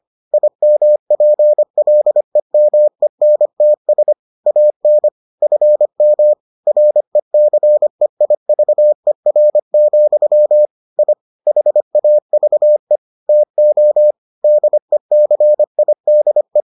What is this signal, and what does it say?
Morse code, heard as a keyed tone, about 25 wpm, 600 Hz. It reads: IMPLEMENTS AN FM RECEIVER, I HAVE TO DECIDE